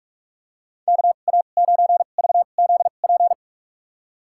Morse code, heard as a keyed tone, about 43 wpm, spaced as for 32 wpm, 700 Hz. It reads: KA9VZP
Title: USA Call Signs - 6 Characters - 2x Character Spacing - 45wpm